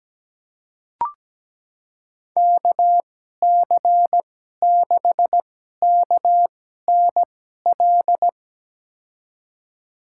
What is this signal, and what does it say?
Morse code, heard as a keyed tone, about 17 wpm, spaced as for 12 wpm, 700 Hz. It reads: KC6KNL
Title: USA Call Signs - 6 Characters - 2x Character Spacing - 17wpm